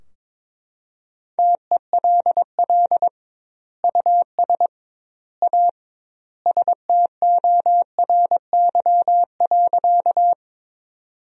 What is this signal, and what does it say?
Morse code, heard as a keyed tone, about 22 wpm, 700 Hz. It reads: TELL US A STORY.
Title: 2x Word-Spacing - Sentences from Top 300 Words - 22wpm